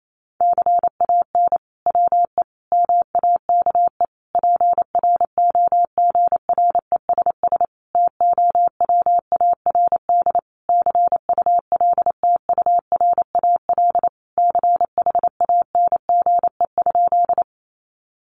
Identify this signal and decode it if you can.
Morse code, heard as a keyed tone, about 28 wpm, 700 Hz. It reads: CAN WE MAKE PROGRESS TOWARD CULTURAL CHANGE?
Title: Sentences from Top 1500 Words 28wpm